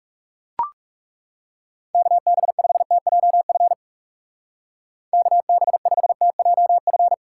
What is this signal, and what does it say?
Morse code, heard as a keyed tone, about 35 wpm, 700 Hz. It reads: KB5TJF KB5TJF
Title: Speed Racing - USA 2x3 Call Signs 30wpm